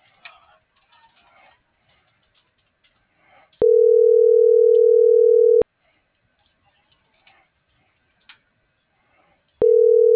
Ringback tone (North America), 440 + 480 Hz.